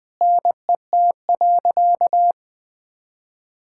Morse code, heard as a keyed tone, about 20 wpm, 700 Hz. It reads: NET.
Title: Sentences with 8-Word Phrases - Spoken Beginning - 20wpm